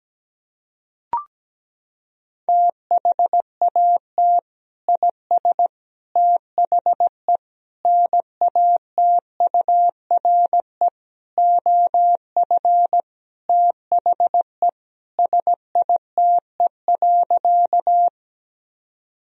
Morse code, heard as a keyed tone, about 17 wpm, 700 Hz. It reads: THAT IS THE NATURE OF THE SITE.
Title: Sentences from Top 1000 Words - Review Part 4 - 17wpm